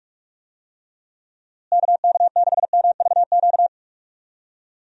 Morse code, heard as a keyed tone, about 43 wpm, 700 Hz. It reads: KK6MVQ
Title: USA Call Signs - 6 Characters - Repeated 3x - 45wpm